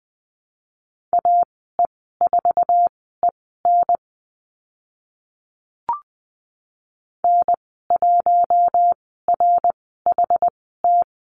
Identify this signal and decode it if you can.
Morse code, heard as a keyed tone, about 20 wpm, spaced as for 15 wpm, 700 Hz. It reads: AE4EN N1RHT
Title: 069 Lesson 12g - Letter d - Call Signs 20wpm